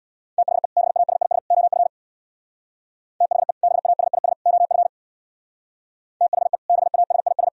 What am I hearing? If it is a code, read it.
Morse code, heard as a keyed tone, about 64 wpm, 700 Hz. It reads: THE BASIS OF THE BASIS OF THE BASIS